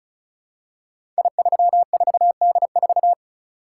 Morse code, heard as a keyed tone, about 36 wpm, 700 Hz. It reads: I34D4
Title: Code Groups - 5 Letter:Numbers - 35wpm